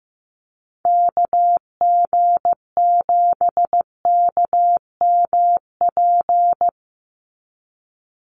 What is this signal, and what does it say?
Morse code, heard as a keyed tone, about 15 wpm, 700 Hz. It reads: KG7KMP
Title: USA Call Signs 15wpm